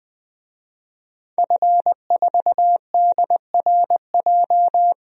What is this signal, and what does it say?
Morse code, heard as a keyed tone, about 21 wpm, 700 Hz. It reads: F4DRJ
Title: DXCC-100 Call Signs - 5 Characters - Repeated 3x - 20wpm